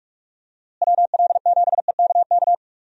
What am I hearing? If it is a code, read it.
Morse code, heard as a keyed tone, about 41 wpm, 700 Hz. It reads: WL7EKK
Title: USA Call Signs - 6 Characters - Repeated 3x - 45wpm